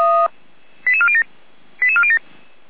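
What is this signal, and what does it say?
Five-tone selective call (ZVEI), heard as quick runs of five tones, tones of about 70 ms. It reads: 88407 88407
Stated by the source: ZVEI five-tone selective call received by SDR.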